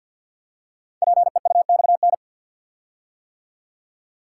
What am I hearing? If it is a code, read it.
Morse code, heard as a keyed tone, about 48 wpm, 700 Hz. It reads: PEUXN
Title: Code Groups - 5 Letters - 50wpm